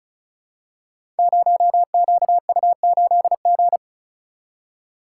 Morse code, heard as a keyed tone, about 35 wpm, 700 Hz. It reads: 0QU8G
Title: Code Groups - 5 Letter:Numbers - 35wpm